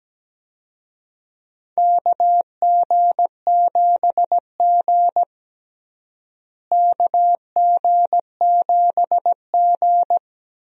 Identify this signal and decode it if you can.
Morse code, heard as a keyed tone, about 17 wpm, 700 Hz. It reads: KG7G KG7G
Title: USA Call Signs - 4 Characters - Repeated 2x - 17wpm